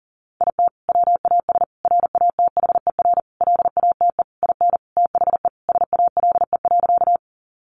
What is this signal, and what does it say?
Morse code, heard as a keyed tone, about 40 wpm, 700 Hz. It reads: IT WAS RATHER LATE IN THE SALE.
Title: Sentences from Top 1400 Words 40wpm